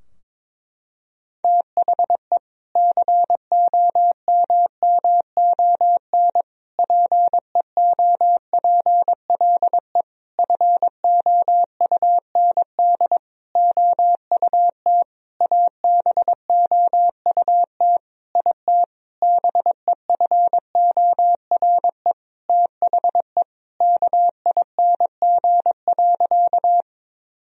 Morse code, heard as a keyed tone, about 22 wpm, 700 Hz. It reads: THE COMMON PEOPLE FOUND OUT ABOUT IT BEFORE THE KING.
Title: Sentences from Top 1000 Words - Review Part 3 - 22wpm